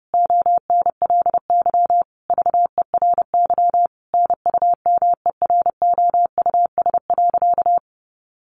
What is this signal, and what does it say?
Morse code, heard as a keyed tone, about 30 wpm, 700 Hz. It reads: ONLY VERY NUMEROUS.
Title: Sentences from Top 1400 Words 30wpm